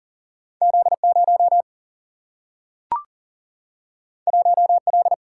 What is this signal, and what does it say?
Morse code, heard as a keyed tone, about 40 wpm, 700 Hz. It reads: Z0 1L
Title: Single-Letter Single-Number 40wpm